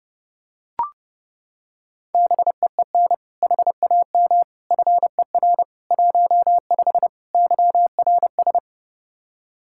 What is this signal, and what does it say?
Morse code, heard as a keyed tone, about 32 wpm, 700 Hz. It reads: BEEN HAM FER 15 YRS
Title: Practice QSO - Exchange 3 - More Info - 30wpm